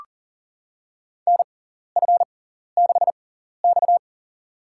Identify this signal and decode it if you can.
Morse code, heard as a keyed tone, about 40 wpm, spaced as for 14 wpm, 700 Hz. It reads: NF6X
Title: USA Call Signs - 4 Characters - 6x Character Spacing - 40wpm